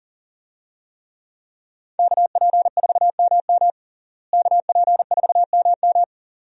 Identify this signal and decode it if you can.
Morse code, heard as a keyed tone, about 40 wpm, 700 Hz. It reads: KP4MM KP4MM
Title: USA Call Signs - 5 Characters - Repeated 2x - 40wpm